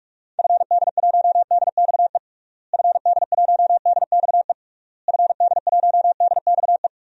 Morse code, heard as a keyed tone, about 46 wpm, 700 Hz. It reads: FD1DXE FD1DXE FD1DXE
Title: DXCC-100 Call Signs - 6 Characters - Repeated 3x - 45wpm